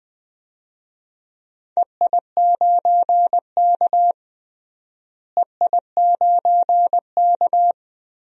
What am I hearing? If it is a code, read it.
Morse code, heard as a keyed tone, about 20 wpm, 700 Hz. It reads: EI9K EI9K
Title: DXCC-100 Call Signs - 4 Characters - Repeated 2x - 20wpm